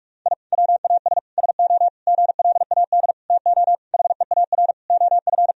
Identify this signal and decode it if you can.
Morse code, heard as a keyed tone, about 47 wpm, 700 Hz. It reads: I WAS SO GLAD TO HEAR OF
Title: Sentences from Top 1000 Words - Review Part 4 - 45wpm